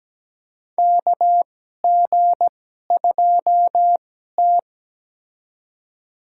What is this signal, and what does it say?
Morse code, heard as a keyed tone, about 17 wpm, spaced as for 12 wpm, 700 Hz. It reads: KG2T
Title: USA Call Signs - 4 Characters - 2x Character Spacing - 17wpm